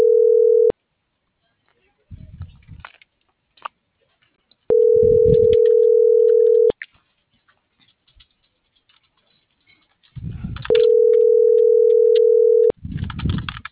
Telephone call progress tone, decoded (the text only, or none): ringback tone